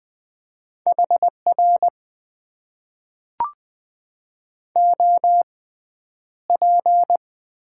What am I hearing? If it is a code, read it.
Morse code, heard as a keyed tone, about 20 wpm, spaced as for 10 wpm, 700 Hz. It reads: HR OP
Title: Top 90 QSO Elements - 6x Character Spacing - 20wpm